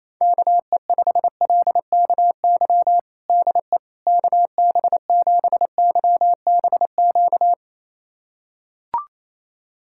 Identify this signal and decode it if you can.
Morse code, heard as a keyed tone, about 28 wpm, 700 Hz. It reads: KE5LKY DE KB7YBQ
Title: USA Callsign DE Callsign - 6 Characters - 28wpm